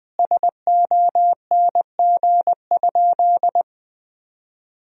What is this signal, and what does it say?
Morse code, heard as a keyed tone, about 20 wpm, 700 Hz. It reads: SONG?
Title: Sentences from Top 1000 Words - Review Part 2 - 20wpm